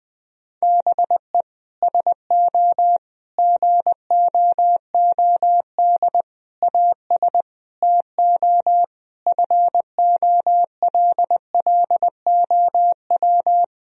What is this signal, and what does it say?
Morse code, heard as a keyed tone, about 20 wpm, 700 Hz. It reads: BE SO GOOD AS TO FOLLOW
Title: Sentences from Top 1000 Words - Review Part 1 - 20wpm